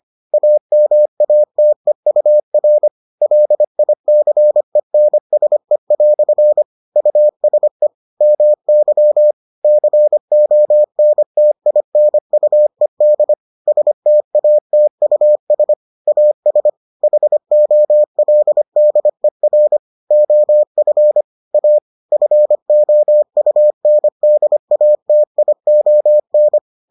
Morse code, heard as a keyed tone, about 25 wpm, 600 Hz. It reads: AMATEUR LICENSE" USE MY CONTINUED STATUS AS HOLDER OF A FOUNDATION